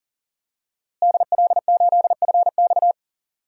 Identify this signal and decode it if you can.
Morse code, heard as a keyed tone, about 40 wpm, 700 Hz. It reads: DL8FX
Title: DXCC-100 Call Signs - 5 Characters - Repeated 3x - 40wpm